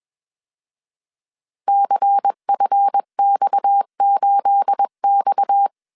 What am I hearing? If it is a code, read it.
Morse code, heard as a keyed tone, about 21 wpm, 800 Hz. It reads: CFX8X